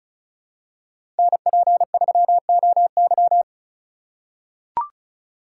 Morse code, heard as a keyed tone, about 34 wpm, 700 Hz. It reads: NP3OY
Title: USA Call Signs - 5 Characters - 3x Character Spacing - 35wpm